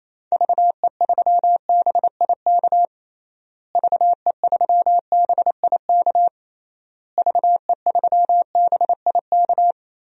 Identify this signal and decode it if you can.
Morse code, heard as a keyed tone, about 28 wpm, 700 Hz. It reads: VE3BIK VE3BIK VE3BIK
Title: DXCC-100 Call Signs - 6 Characters - Repeated 3x - 28wpm